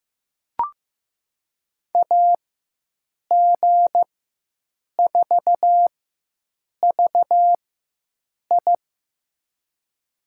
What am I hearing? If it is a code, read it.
Morse code, heard as a keyed tone, about 15 wpm, spaced as for 7 wpm, 700 Hz. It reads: AG4VI
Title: USA Call Signs - 5 Characters - 4x Character Spacing - 15wpm